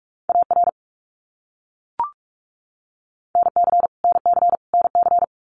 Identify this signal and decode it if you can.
Morse code, heard as a keyed tone, about 44 wpm, 700 Hz. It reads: AR NC NC NC
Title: US States Repeated 3x 45wpm